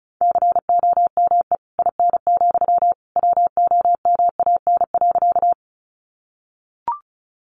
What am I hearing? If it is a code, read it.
Morse code, heard as a keyed tone, about 34 wpm, 700 Hz. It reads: COME IN, WOMAN.
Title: Sentences from Top 1100 Words 35wpm